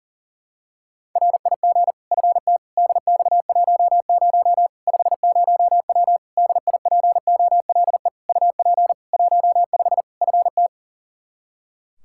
Morse code, heard as a keyed tone, about 40 wpm, 700 Hz. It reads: RIG FT DX10 50W DIPOLE UP 15 FT
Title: K1USN SST QSO Practice 40wpm